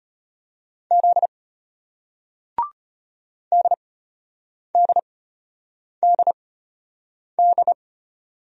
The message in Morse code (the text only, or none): Z DDDD